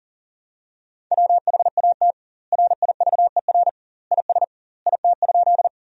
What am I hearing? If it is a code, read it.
Morse code, heard as a keyed tone, about 40 wpm, 700 Hz. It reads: WHAT RIVER IS IT?
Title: 2x Word-Spacing - Sentences from Top 300 Words - 40wpm